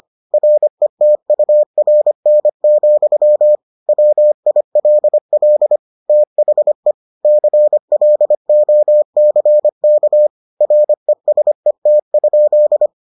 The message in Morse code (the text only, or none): RETURN, WILL THE CLOCK RESET?